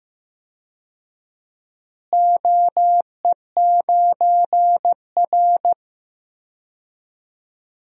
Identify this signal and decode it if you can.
Morse code, heard as a keyed tone, about 15 wpm, 700 Hz. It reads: OE9R
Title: DXCC-100 Call Signs - 4 Characters - Repeated 3x - 15wpm